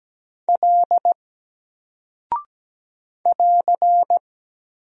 Morse code, heard as a keyed tone, about 17 wpm, 700 Hz. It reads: L <AR>
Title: Single Character (Letter:Number:Punctuation:Prosign) 17wpm